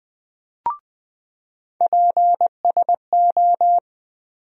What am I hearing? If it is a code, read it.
Morse code, heard as a keyed tone, about 20 wpm, 700 Hz. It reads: PSO
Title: USA Call Sign Postfixes - 3 Letters 20wpm